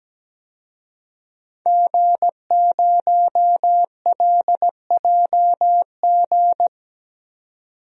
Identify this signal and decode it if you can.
Morse code, heard as a keyed tone, about 17 wpm, 700 Hz. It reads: G0LJG
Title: DXCC-100 Call Signs - 5 Characters - 17wpm